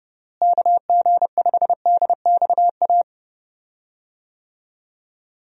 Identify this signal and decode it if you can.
Morse code, heard as a keyed tone, about 30 wpm, 700 Hz. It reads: KG5DXA
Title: USA Call Signs - 6 Characters - Repeated 2x - 30wpm